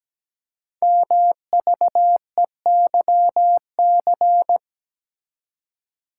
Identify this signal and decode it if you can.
Morse code, heard as a keyed tone, about 17 wpm, 700 Hz. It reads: MVEYC
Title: Code Groups - 5 Letters - 17wpm